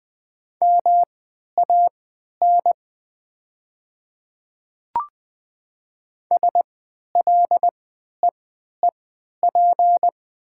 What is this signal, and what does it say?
Morse code, heard as a keyed tone, about 20 wpm, spaced as for 11 wpm, 700 Hz. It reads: MAN SLEEP
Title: 3x Character Spacing - Top 500 Words - 20wpm